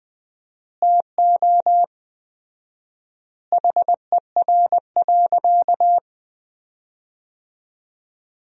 Morse code, heard as a keyed tone, about 20 wpm, 700 Hz. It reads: TO HER.
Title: 4x Word-Spacing - Sentences from Top 300 Words - 20wpm